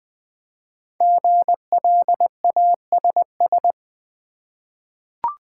Morse code, heard as a keyed tone, about 20 wpm, 700 Hz. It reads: GLASS
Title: Five Letter Words - 20wpm